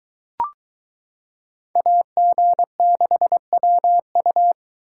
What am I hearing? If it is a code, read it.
Morse code, heard as a keyed tone, about 22 wpm, 700 Hz. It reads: AG6WU